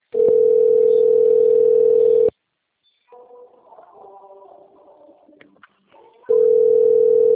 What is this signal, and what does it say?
Ringback tone (North America), 440 + 480 Hz.